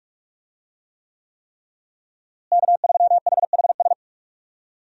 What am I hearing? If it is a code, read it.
Morse code, heard as a keyed tone, about 45 wpm, 700 Hz. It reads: K3HHS
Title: USA Call Signs - 5 Characters - Repeated 3x - 45wpm